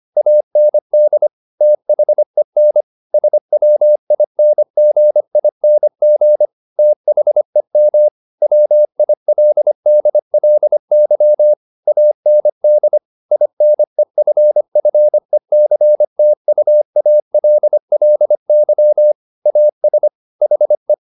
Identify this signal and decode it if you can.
Morse code, heard as a keyed tone, about 25 wpm, 600 Hz. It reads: AND THEN SWINGING THEM WILDLY AND INEFFECTUALLY AS HE
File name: the old man and the sea part 1 25wpm